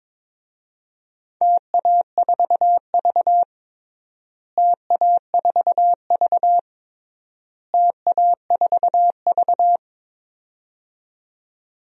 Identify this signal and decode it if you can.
Morse code, heard as a keyed tone, about 22 wpm, 700 Hz. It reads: TA4V TA4V TA4V
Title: DXCC-100 Call Signs - 4 Characters - Repeated 3x - 22wpm